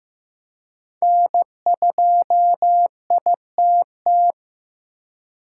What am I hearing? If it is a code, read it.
Morse code, heard as a keyed tone, about 15 wpm, 700 Hz. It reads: N2ITT
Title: USA Call Signs - 5 Characters - 4x Character Spacing - 15wpm